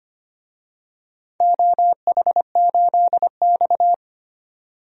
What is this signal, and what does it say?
Morse code, heard as a keyed tone, about 25 wpm, 700 Hz. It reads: OH8X
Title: DXCC-100 Call Signs - 4 Characters - 25wpm